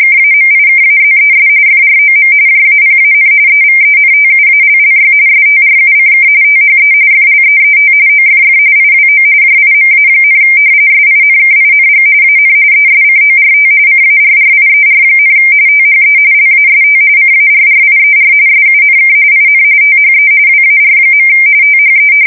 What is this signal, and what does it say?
RTTY, about 45.45 baud. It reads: KAMIKAZES FROM JAPAN EVENTUALLY STRUCK THE FLEET OFF IWO JIMA, SINKING AN ESCORT CARRIER, THE LAST CARRIER SUNK IN THE WAR. THE
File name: fsk170 / battleshipnj